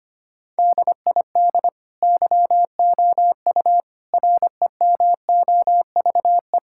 Morse code, heard as a keyed tone, about 25 wpm, 700 Hz. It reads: DID YOU REMOVE
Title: Sentences from Top 1000 Words - Review Part 5 - 25wpm